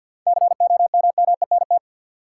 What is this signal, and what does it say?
Morse code, heard as a keyed tone, about 53 wpm, 700 Hz. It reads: COMMENT
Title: Seven Letter Words - 50wpm